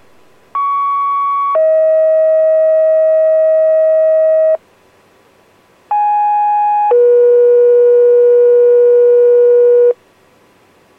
Two-tone page, heard as a one-second tone, then a three-second tone, about 1120 then 640 Hz, about 830 then 480 Hz. It is heard from 0.5 to 9.9 s.